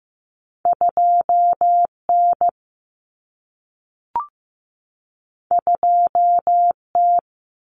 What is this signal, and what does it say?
Morse code, heard as a keyed tone, about 15 wpm, 700 Hz. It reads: 2N 2T
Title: Common Cut Numbers - 2-Digit 15wpm